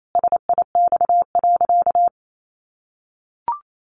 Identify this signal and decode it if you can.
Morse code, heard as a keyed tone, about 28 wpm, 700 Hz. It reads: SIX.
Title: Sentences from Top 600 Words 28wpm